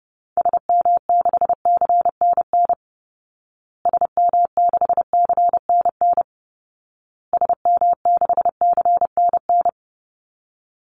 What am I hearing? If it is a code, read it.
Morse code, heard as a keyed tone, about 30 wpm, 700 Hz. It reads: SM6CNN SM6CNN SM6CNN
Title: Call Signs Repeated 2x - 30wpm